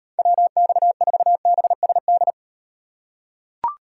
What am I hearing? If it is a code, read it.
Morse code, heard as a keyed tone, about 37 wpm, 700 Hz. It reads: WX4BSD